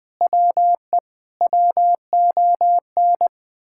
Morse code, heard as a keyed tone, about 20 wpm, 700 Hz. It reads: WE WON
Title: Sentences with 8-Word Phrases - 20wpm